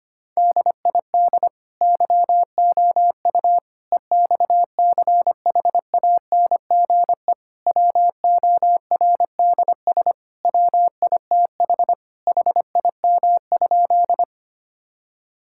Morse code, heard as a keyed tone, about 25 wpm, 700 Hz. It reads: DID YOU EXCHANGE WORDS WITH HIM?
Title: Sentences from Top 1000 Words - Review Part 6 - 25wpm